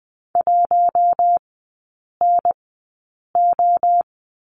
Morse code, heard as a keyed tone, about 20 wpm, spaced as for 8 wpm, 700 Hz. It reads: 1NO
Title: 067 Lesson 12e - Letter d - Three Characters 20wpm